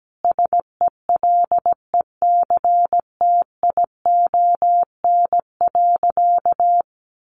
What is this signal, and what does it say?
Morse code, heard as a keyed tone, about 17 wpm, 700 Hz. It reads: SELECTION.
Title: Sentences from Top 1400 Words 17wpm